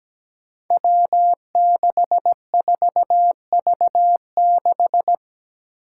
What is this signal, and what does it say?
Morse code, heard as a keyed tone, about 17 wpm, 700 Hz. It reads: W64V6
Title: Five Challenging Letter-Number Combinations 17wpm